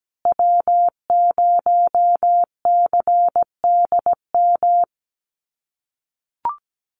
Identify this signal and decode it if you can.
Morse code, heard as a keyed tone, about 17 wpm, 700 Hz. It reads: W0CDM
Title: USA Call Signs Repeated 3x 17wpm